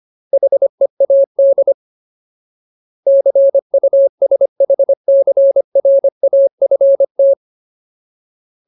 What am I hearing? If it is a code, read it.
Morse code, heard as a keyed tone, about 25 wpm, 550 Hz. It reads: HEAD CUSHCRAFT
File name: Ultimate Word List / 25WPM Ultimate